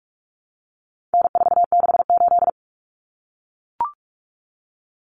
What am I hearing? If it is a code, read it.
Morse code, heard as a keyed tone, about 44 wpm, 700 Hz. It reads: N468
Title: Common Cut Numbers - 4-Digit 45wpm